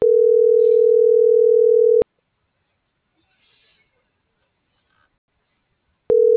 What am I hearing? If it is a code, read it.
Ringback tone (North America), 440 + 480 Hz.